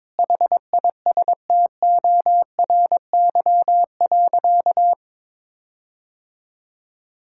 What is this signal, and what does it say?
Morse code, heard as a keyed tone, about 22 wpm, 700 Hz. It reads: HISTORY.
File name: practice / 22wpm / Sentences with 8-Word Phrases - 22wpm